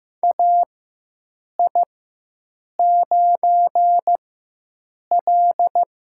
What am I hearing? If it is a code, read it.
Morse code, heard as a keyed tone, about 15 wpm, spaced as for 7 wpm, 700 Hz. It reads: AI9L
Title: USA Call Signs - 4 Characters - 4x Character Spacing - 15wpm